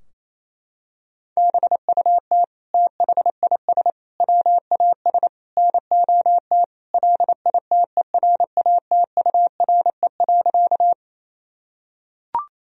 Morse code, heard as a keyed tone, about 28 wpm, 700 Hz. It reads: BUT THIS WAS NOT LITERATURE.
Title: Sentences from Top 1000 Words - Review Part 3 - 28wpm